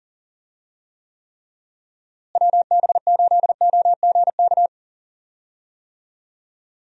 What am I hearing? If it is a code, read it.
Morse code, heard as a keyed tone, about 40 wpm, 700 Hz. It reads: WB8OGK
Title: USA Call Signs - 6 Characters - Repeated 3x - 40wpm